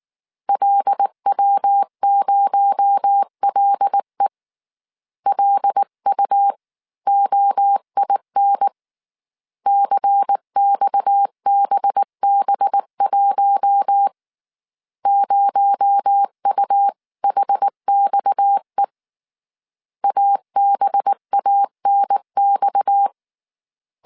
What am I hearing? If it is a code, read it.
Morse code, heard as a keyed tone, about 19 wpm, 800 Hz. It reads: LW0LE LUOIN CXBB1 0UHXE ABANX
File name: CW / DET / 1 6 1500C S C 18wpm 2356